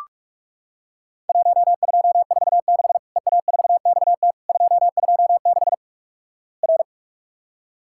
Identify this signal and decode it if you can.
Morse code, heard as a keyed tone, about 46 wpm, 700 Hz. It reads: 1246 EA4XT 226 R
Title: Worked All Europe DX Contest - QTC Exchange - 45wpm